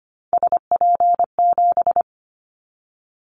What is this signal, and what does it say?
Morse code, heard as a keyed tone, about 25 wpm, 700 Hz. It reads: SP7
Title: Callsign Prefixes for Top 25 DXCC - 25wpm